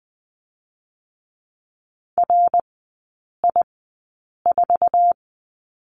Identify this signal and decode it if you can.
Morse code, heard as a keyed tone, about 20 wpm, spaced as for 8 wpm, 690 Hz. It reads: RI4
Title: 093 Lesson 16e - Letter c - Three Characters 20wpm